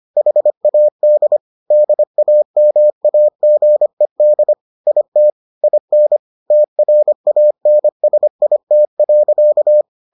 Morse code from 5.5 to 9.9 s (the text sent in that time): IN TRANSIT.